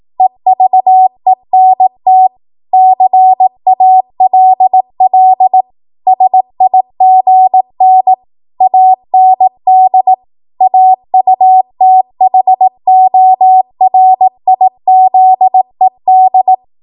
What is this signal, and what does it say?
Morse code, heard as a keyed tone, about 18 wpm, 750 Hz. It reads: EVENT CALL SIGN AND AUTHORIZED
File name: Morse / 18WPM